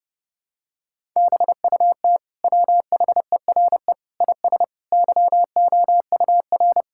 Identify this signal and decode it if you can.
Morse code, heard as a keyed tone, about 32 wpm, 700 Hz. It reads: BUT WHERE IS YOUR
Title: Sentences from Top 1000 Words - Review Part 5 - 30wpm